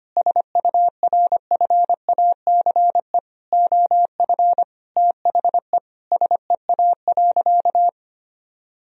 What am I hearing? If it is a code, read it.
Morse code, heard as a keyed tone, about 25 wpm, 700 Hz. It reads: SURFACE OF THE SEA.
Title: Sentences from Top 1000 Words - Review Part 3 - 25wpm